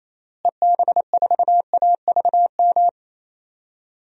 Morse code, heard as a keyed tone, about 27 wpm, 700 Hz. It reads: EB4AVM
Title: DXCC-100 Call Signs - 6 Characters - Repeated 2x - 28wpm